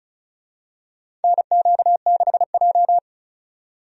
Morse code, heard as a keyed tone, about 34 wpm, 700 Hz. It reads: NQ6J
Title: USA Call Signs - 4 Characters - 35wpm